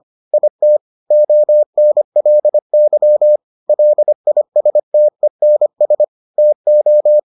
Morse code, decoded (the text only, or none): IT ONLY LISTENS TO